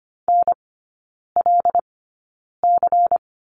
Morse code, heard as a keyed tone, about 25 wpm, spaced as for 9 wpm, 700 Hz. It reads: NLC
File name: Lesson 17e - Letter u - Three Characters 25wpm